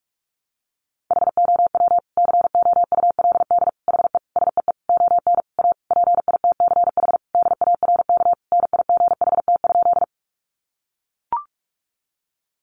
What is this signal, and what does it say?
Morse code, heard as a keyed tone, about 44 wpm, 700 Hz. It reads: HOW COULD HE SEE ON A PITCH DARK NIGHT?
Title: Sentences from Top 1300 Words 45wpm